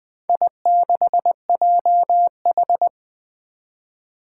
Morse code, heard as a keyed tone, about 20 wpm, 700 Hz. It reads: I6JH
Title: Four Challenging Letter-Number Combinations 20wpm